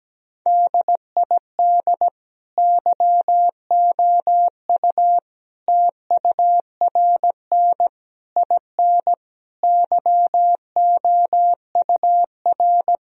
Morse code, heard as a keyed tone, about 17 wpm, 700 Hz. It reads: DID YOU TURN IN YOUR
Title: Sentences from Top 1000 Words - Review Part 2 - 17wpm